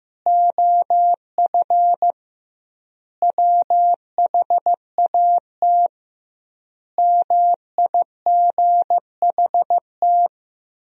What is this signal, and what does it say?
Morse code, heard as a keyed tone, about 15 wpm, 700 Hz. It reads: OF WHAT MIGHT
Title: 2x Word-Spacing - Sentences from Top 400 Words - 15wpm